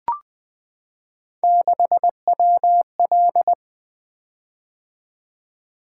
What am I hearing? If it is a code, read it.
Morse code, heard as a keyed tone, about 20 wpm, 700 Hz. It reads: 6WL
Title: Three Letter-Number Combinations - 20wpm